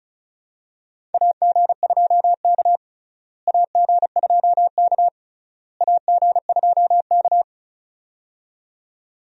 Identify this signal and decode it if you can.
Morse code, heard as a keyed tone, about 36 wpm, 700 Hz. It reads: AG2K AG2K AG2K
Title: USA Call Signs - 4 Characters - Repeated 3x - 35wpm